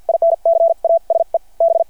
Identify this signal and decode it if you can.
Morse code, heard as a keyed tone, about 50 wpm, spaced as for 42 wpm, 670 Hz. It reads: IN CASE B
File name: archived / CRSnet